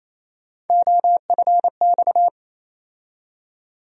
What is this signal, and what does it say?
Morse code, heard as a keyed tone, about 27 wpm, 700 Hz. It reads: OFX
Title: USA Call Sign Postfixes - 3 Letters 28wpm